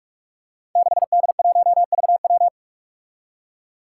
Morse code, heard as a keyed tone, about 42 wpm, 700 Hz. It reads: 6D1VW
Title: Five Challenging Letter-Number Combinations 45wpm